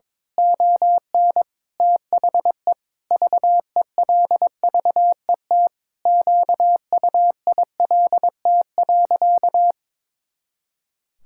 Morse code, heard as a keyed tone, about 22 wpm, 700 Hz. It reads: ON THE VELVET QUILT.
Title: Pangram Challenge v2 - 22wpm